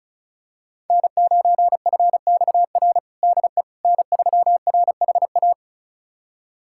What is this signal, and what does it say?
Morse code, heard as a keyed tone, about 36 wpm, 700 Hz. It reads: N9FXR DE N3RHA
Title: USA Callsign DE Callsign - 5 Characters - 35wpm